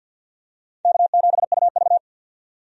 Morse code, heard as a keyed tone, about 53 wpm, 700 Hz. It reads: K7UV
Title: USA Call Signs - 4 Characters - 50wpm